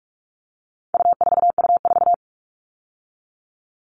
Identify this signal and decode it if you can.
Morse code, heard as a keyed tone, about 44 wpm, 700 Hz. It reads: U4U4
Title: All Cut Numbers - 4-Digit 45wpm